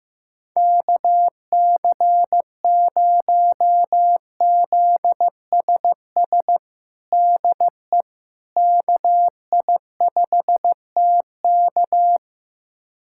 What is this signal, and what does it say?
Morse code, heard as a keyed tone, about 16 wpm, 700 Hz. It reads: KC0ZSS DE KI5TK
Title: USA Callsign DE Callsign 15wpm